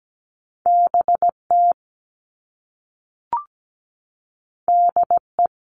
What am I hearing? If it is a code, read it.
Morse code, heard as a keyed tone, about 17 wpm, 700 Hz. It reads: BT DE
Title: All Cut Numbers - 2-Digit 17wpm